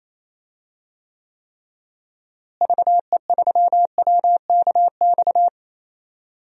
Morse code, heard as a keyed tone, about 28 wpm, 700 Hz. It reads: VE3WKX
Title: DXCC-100 Call Signs - 6 Characters - Repeated 3x - 28wpm